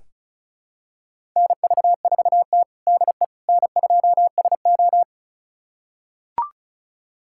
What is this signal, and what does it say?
Morse code, heard as a keyed tone, about 36 wpm, 700 Hz. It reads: NV4T DE N2SO
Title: USA Callsign DE Callsign - 4 Characters - 35wpm